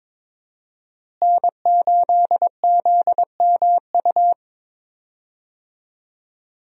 Morse code, heard as a keyed tone, about 22 wpm, 700 Hz. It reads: N8ZMU
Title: USA Call Signs - 5 Characters - Repeated 3x - 22wpm